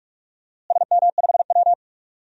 Morse code, heard as a keyed tone, about 48 wpm, 700 Hz. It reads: SM5W
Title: DXCC-100 Call Signs - 4 Characters - Repeated 3x - 45wpm